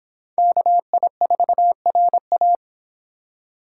Morse code, heard as a keyed tone, about 25 wpm, 700 Hz. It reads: KI4RA